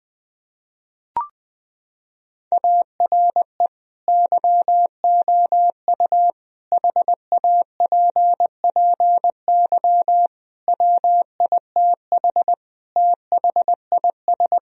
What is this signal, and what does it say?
Morse code, heard as a keyed tone, about 20 wpm, 700 Hz. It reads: ARE YOU HAPPY WITH THIS
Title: Sentences from Top 1000 Words - Review Part 2 - 20wpm